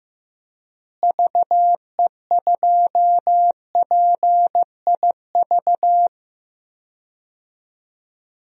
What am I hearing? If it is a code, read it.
Morse code, heard as a keyed tone, about 15 wpm, 700 Hz. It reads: VE2PIV
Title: DXCC-100 Call Signs - 6 Characters - Repeated 3x - 15wpm